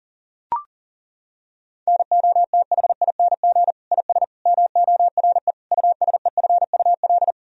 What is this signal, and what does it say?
Morse code, heard as a keyed tone, about 40 wpm, 700 Hz. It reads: NOTHING IS MORE USEFUL
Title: Sentences from Top 1000 Words - Review Part 4 - 40wpm